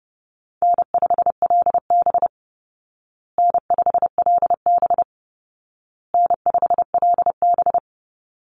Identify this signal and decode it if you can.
Morse code, heard as a keyed tone, about 30 wpm, 700 Hz. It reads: N5LB N5LB N5LB
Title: Easy Call Signs Repeated 3x - 30wpm